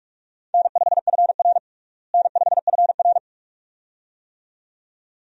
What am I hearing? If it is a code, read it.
Morse code, heard as a keyed tone, about 46 wpm, 700 Hz. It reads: N5FR N5FR
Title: USA Call Signs - 4 Characters - Repeated 2x - 45wpm